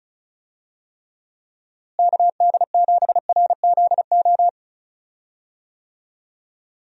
Morse code, heard as a keyed tone, about 36 wpm, 700 Hz. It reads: KD7RZO
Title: USA Call Signs - 6 Characters - Repeated 3x - 35wpm